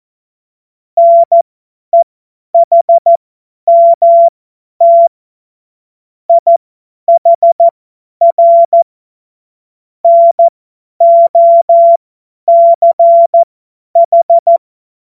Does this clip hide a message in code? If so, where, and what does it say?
Yes: Morse code from 1.0 to 14.6 s, heard as a keyed tone, about 14 wpm, spaced as for 10 wpm, 680 Hz. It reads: NEHMT IHR NOCH